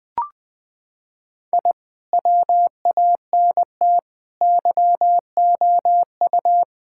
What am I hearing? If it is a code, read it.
Morse code, heard as a keyed tone, about 19 wpm, 700 Hz. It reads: I WANT YOU
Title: Sentences with 2-Word Phrases - 20wpm